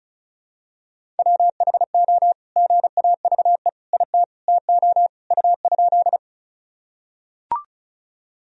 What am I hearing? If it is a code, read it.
Morse code, heard as a keyed tone, about 34 wpm, 700 Hz. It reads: WHO GAVE IT TO U?
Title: Sentences from Top 500 Words and Abbreviations 35wpm